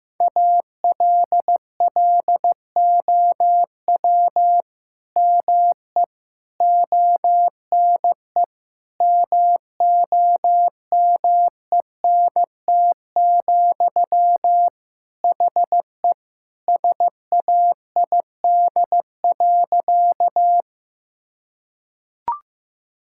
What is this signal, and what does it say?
Morse code, heard as a keyed tone, about 15 wpm, 700 Hz. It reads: ALLOW ME ONE MOMENT, HE SAID.
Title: Sentences from Top 1000 Words - Review Part 5 - 15wpm